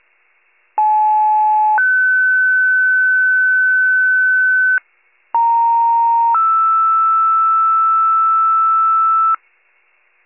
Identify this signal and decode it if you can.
Two-tone page, heard as a one-second tone, then a three-second tone, about 860 then 1500 Hz, about 930 then 1320 Hz. It is heard from 0.8 to 9.3 s.